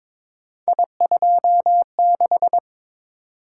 Morse code, heard as a keyed tone, about 22 wpm, 700 Hz. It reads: I26
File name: practice / 22wpm / Three Letter-Number Combinations - 22wpm